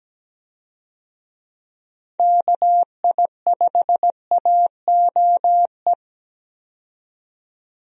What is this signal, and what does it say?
Morse code, heard as a keyed tone, about 17 wpm, 700 Hz. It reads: KI5AOE
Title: USA Call Signs - 6 Characters - 17wpm